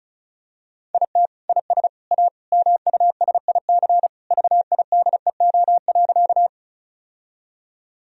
Morse code, heard as a keyed tone, about 36 wpm, 700 Hz. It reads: IT IS A MUSIC VIDEO.
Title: Sentences from Top 1000 Words - Review Part 4 - 35wpm